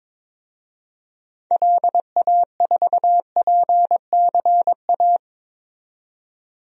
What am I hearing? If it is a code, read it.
Morse code, heard as a keyed tone, about 22 wpm, 700 Hz. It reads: LA4PCA
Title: DXCC-100 Call Signs - 6 Characters - 22wpm